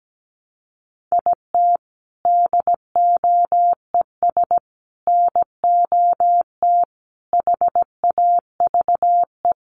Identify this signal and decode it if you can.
Morse code, heard as a keyed tone, about 17 wpm, 700 Hz. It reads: IT DOES NOT HAVE
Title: Sets of 5 Words 17wpm